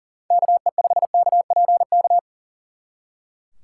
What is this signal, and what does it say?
Morse code, heard as a keyed tone, about 38 wpm, 700 Hz. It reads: KE5KPK
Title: Speed Racing - USA Call Signs 40wpm